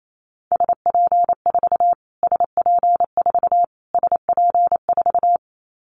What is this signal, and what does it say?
Morse code, heard as a keyed tone, about 28 wpm, 700 Hz. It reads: SP4 SP4 SP4
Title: Callsign Prefixes for Top 25 DXCC - Repeated 3x - 28wpm